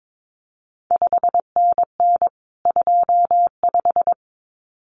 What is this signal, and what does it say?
Morse code, heard as a keyed tone, about 22 wpm, 700 Hz. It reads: 5NN 25
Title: CQ WW DX Contest Exchange Encore - 22wpm